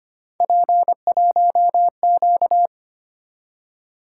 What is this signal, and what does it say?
Morse code, heard as a keyed tone, about 29 wpm, 700 Hz. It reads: P1Q
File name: Speed-Racing - Three Letter-Number Combinations - 25wpm